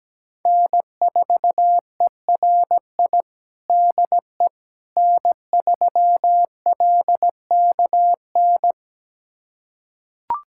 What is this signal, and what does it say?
Morse code, heard as a keyed tone, about 17 wpm, 700 Hz. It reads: N4ERI DE N3LKN
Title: USA Callsign DE Callsign - 5 Characters - 17wpm